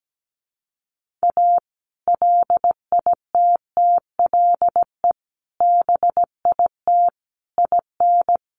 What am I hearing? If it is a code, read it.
Morse code, heard as a keyed tone, about 17 wpm, 700 Hz. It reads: A LITTLE BIT IN
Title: Sentences from Top 900 Words 17wpm